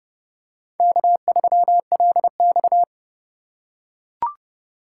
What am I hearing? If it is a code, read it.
Morse code, heard as a keyed tone, about 30 wpm, 700 Hz. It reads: K3LX
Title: USA Call Signs - 4 Characters - 6x Character Spacing - 30wpm